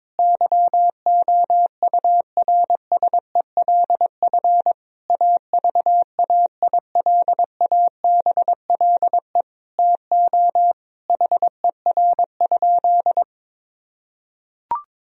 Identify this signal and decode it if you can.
Morse code, heard as a keyed tone, about 22 wpm, 700 Hz. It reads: YOURSELF AVAILABLE TO HER?
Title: Sentences from Top 1000 Words - Review Part 3 - 22wpm